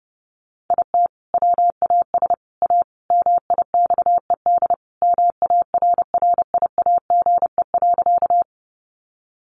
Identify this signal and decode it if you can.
Morse code, heard as a keyed tone, about 30 wpm, 700 Hz. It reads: IT WAS A MIXED MARRIAGE.
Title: Sentences from Top 1500 Words 30wpm